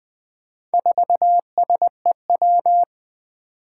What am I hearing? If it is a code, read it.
Morse code, heard as a keyed tone, about 20 wpm, 700 Hz. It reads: 4SEW
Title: Four Challenging Letter-Number Combinations 20wpm